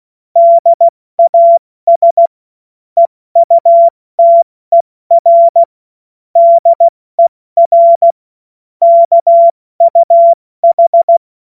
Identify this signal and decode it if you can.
Morse code, heard as a keyed tone, about 16 wpm, 680 Hz. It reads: DAS EUTER DER KUH
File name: Morsekurs / Witze / Witz 24 Juni Mai 2025 70 F80